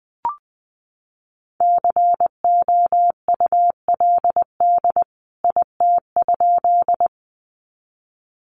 Morse code, heard as a keyed tone, about 20 wpm, 700 Hz. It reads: COULD IT?
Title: A Sound Of Thunder 20wpm